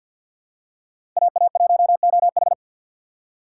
Morse code, heard as a keyed tone, about 48 wpm, 700 Hz. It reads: AA1OH
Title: USA Call Signs - 4x Character Spacing - 50wpm